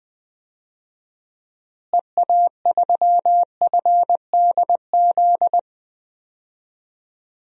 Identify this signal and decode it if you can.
Morse code, heard as a keyed tone, about 21 wpm, 700 Hz. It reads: EA3FDZ
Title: DXCC-100 Call Signs - 6 Characters - Repeated 3x - 20wpm